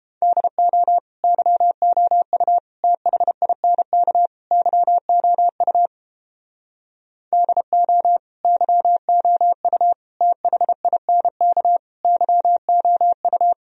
Morse code, heard as a keyed tone, about 31 wpm, 700 Hz. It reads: DO YOU THINK YOU DO YOU THINK YOU